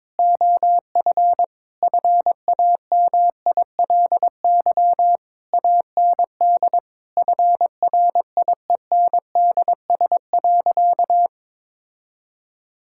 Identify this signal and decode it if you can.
Morse code, heard as a keyed tone, about 22 wpm, 700 Hz. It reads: OF FAMILY AND FRIENDS.
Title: Sentences from Top 1000 Words - Review Part 5 - 22wpm